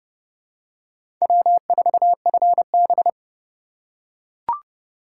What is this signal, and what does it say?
Morse code, heard as a keyed tone, about 30 wpm, 700 Hz. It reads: W4FB
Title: USA Call Signs - 4 Characters - 4x Character Spacing - 30wpm